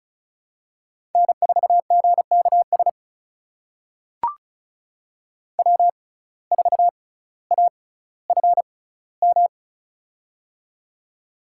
Morse code, heard as a keyed tone, about 36 wpm, spaced as for 18 wpm, 700 Hz. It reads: N4GKS W4AFM
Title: USA Call Signs - 5 Characters - 6x Character Spacing - 35wpm